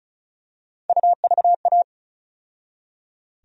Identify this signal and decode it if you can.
Morse code, heard as a keyed tone, about 34 wpm, 700 Hz. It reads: UVA
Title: USA Call Sign Postfixes - 3 Letters 35wpm